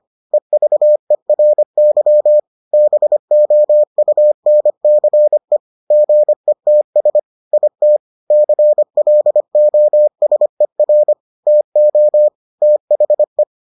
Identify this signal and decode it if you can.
Morse code, heard as a keyed tone, about 25 wpm, 600 Hz. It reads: EVERY BOUNCE GETS IT CLOSER TO THE